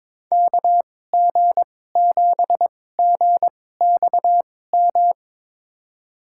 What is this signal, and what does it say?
Morse code, heard as a keyed tone, about 22 wpm, spaced as for 16 wpm, 700 Hz. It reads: KG7GXM
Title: USA Call Signs - 6 Characters - 2x Character Spacing - 22wpm